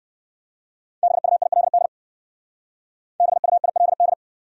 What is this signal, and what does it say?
Morse code, heard as a keyed tone, about 60 wpm, 700 Hz. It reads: BUILD BUILD